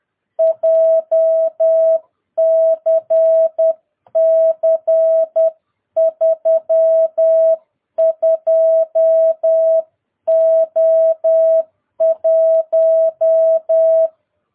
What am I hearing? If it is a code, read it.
Morse code, heard as a keyed tone, about 10 wpm, 650 Hz. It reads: JCC 32O1
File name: jh8cbh / 2022cw